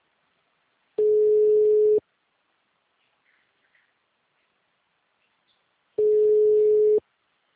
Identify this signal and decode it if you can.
Ringback tone (Europe, 425 Hz).
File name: out- -2012- - - .42